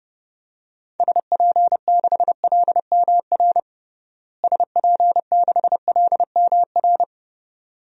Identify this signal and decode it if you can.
Morse code, heard as a keyed tone, about 30 wpm, 700 Hz. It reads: SP6LMR SP6LMR
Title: DXCC-100 Call Signs - 6 Characters - Repeated 2x - 30wpm